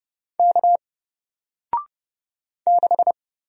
Morse code, heard as a keyed tone, about 30 wpm, 700 Hz. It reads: K 6
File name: Single-Letter-Number-30wpm